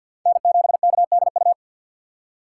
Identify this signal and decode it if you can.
Morse code, heard as a keyed tone, about 53 wpm, 700 Hz. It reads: N7KDU
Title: USA Call Signs - 5 Characters - 6x Character Spacing - 50wpm